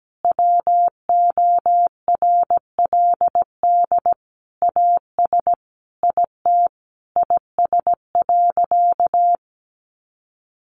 Morse code, heard as a keyed tone, about 17 wpm, 700 Hz. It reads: WORLD AS IT IS.
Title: Sentences from Top 1200 Words 17wpm